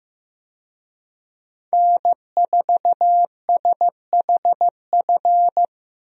Morse code, heard as a keyed tone, about 15 wpm, 700 Hz. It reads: N4SHF
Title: USA Call Signs - 5 Characters - Repeated 2x - 15wpm